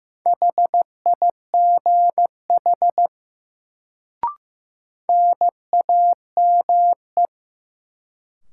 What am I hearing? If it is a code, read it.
Morse code, heard as a keyed tone, about 15 wpm, 700 Hz. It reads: HIGH NAME
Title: Sentences from Top 200 Words - New Word Drill - 15wpm